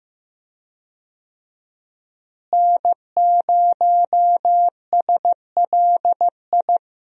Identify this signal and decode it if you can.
Morse code, heard as a keyed tone, about 15 wpm, 700 Hz. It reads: N0SLI
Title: USA Call Signs - 5 Characters - Repeated 2x - 15wpm